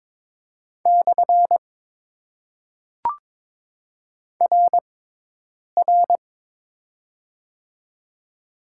Morse code, heard as a keyed tone, about 22 wpm, 700 Hz. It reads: / R R
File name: Top 90 QSO Elements - 6x Character Spacing - 22wpm